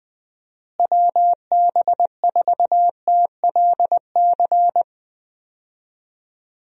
Morse code, heard as a keyed tone, about 21 wpm, 700 Hz. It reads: WB4TLC
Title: USA Call Signs - 6 Characters - Repeated 2x - 20wpm